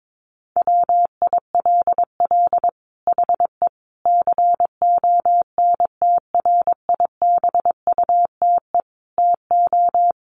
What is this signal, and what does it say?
Morse code, heard as a keyed tone, about 22 wpm, 700 Hz. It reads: WILL HE CONTRIBUTE TO
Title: Sentences from Top 1400 Words 22wpm